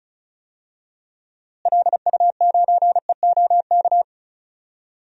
Morse code, heard as a keyed tone, about 36 wpm, 700 Hz. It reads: LU9EOK
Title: DXCC-100 Call Signs - 6 Characters - Repeated 2x - 35wpm